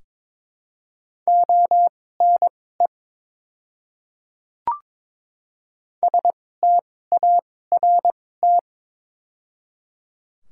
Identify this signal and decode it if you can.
Morse code, heard as a keyed tone, about 22 wpm, spaced as for 16 wpm, 700 Hz. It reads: ONE START
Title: 2x Character Spacing - Top 300 Words - 22wpm